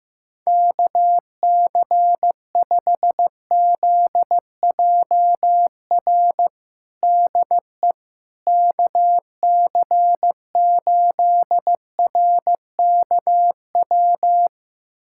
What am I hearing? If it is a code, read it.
Morse code, heard as a keyed tone, about 16 wpm, 700 Hz. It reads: KC5ZJR DE KC8RKW
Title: USA Callsign DE Callsign - 6 Characters - 15wpm